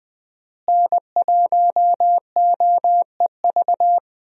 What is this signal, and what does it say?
Morse code, heard as a keyed tone, about 20 wpm, 700 Hz. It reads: N1OEV
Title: USA Call Signs - 5 Characters - Repeated 3x - 20wpm